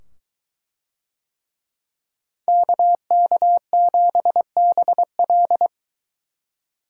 Morse code, heard as a keyed tone, about 22 wpm, 700 Hz. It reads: KK7BL